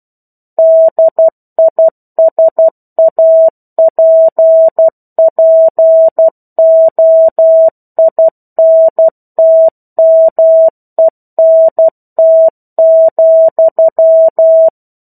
Morse code, heard as a keyed tone, about 12 wpm, 650 Hz. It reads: DISAPPOINTMENT,
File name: 01-Chapter-01 12 WPM